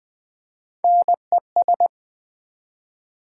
Morse code, heard as a keyed tone, about 20 wpm, 700 Hz. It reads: NES
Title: Build A Word - 1 Letter At A Time - 4-Letter Words - 20wpm